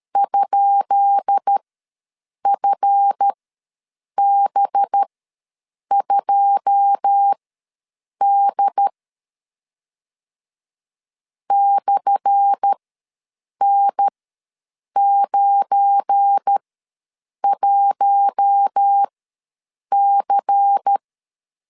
Morse code, heard as a keyed tone, about 13 wpm, spaced as for 7 wpm, 800 Hz. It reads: ?FB2D /N91C
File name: LEZIONI / 10 SIGNS / 10D COMA ws 12wpm 3618